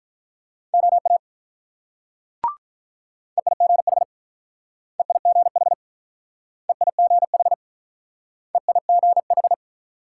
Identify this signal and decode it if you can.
Morse code, heard as a keyed tone, about 46 wpm, 700 Hz. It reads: OA EIGH EIGH EIGH EIGH